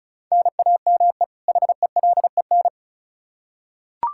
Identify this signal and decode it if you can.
Morse code, heard as a keyed tone, about 34 wpm, 700 Hz. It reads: NAME HELEN
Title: Practice QSO - Exchange 1 - Beginning - 35wpm